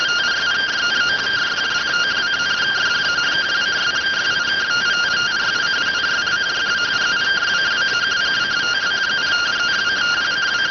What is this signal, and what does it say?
Two-tone teleprinter signal, about 170 Hz shift.